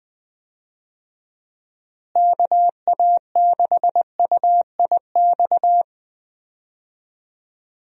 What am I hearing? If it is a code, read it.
Morse code, heard as a keyed tone, about 20 wpm, 700 Hz. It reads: KA6UIX
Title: USA Call Signs - 6 Characters - Repeated 2x - 20wpm